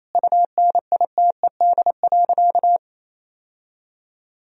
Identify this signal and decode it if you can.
Morse code, heard as a keyed tone, about 27 wpm, 700 Hz. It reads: UNITED.
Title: Sentences from Top 1000 Words - Review Part 4 - 28wpm